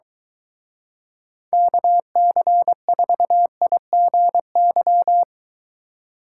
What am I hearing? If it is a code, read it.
Morse code, heard as a keyed tone, about 23 wpm, 700 Hz. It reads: KC4IGY